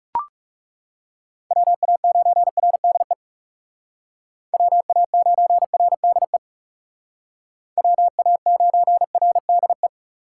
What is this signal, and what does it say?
Morse code, heard as a keyed tone, about 39 wpm, 700 Hz. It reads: WA9RDE WA9RDE WA9RDE